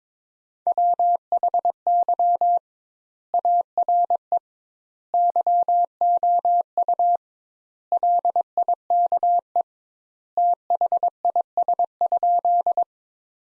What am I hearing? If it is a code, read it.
Morse code, heard as a keyed tone, about 22 wpm, 700 Hz. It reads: WHY ARE YOU LIKE THIS?
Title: Sentences Top 200 Words - 22wpm